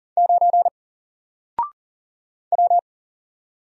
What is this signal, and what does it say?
Morse code, heard as a keyed tone, about 40 wpm, 700 Hz. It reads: 9 W
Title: Single Letter-Number 40wpm